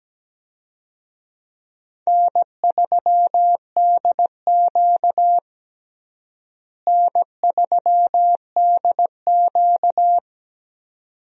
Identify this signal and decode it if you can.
Morse code, heard as a keyed tone, about 17 wpm, 700 Hz. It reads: N3DQ N3DQ
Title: USA Call Signs - 4 Characters - Repeated 2x - 17wpm